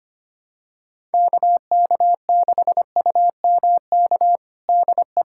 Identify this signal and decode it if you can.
Morse code, heard as a keyed tone, about 25 wpm, 700 Hz. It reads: KK6UMK DE
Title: USA Callsign DE Callsign 25wpm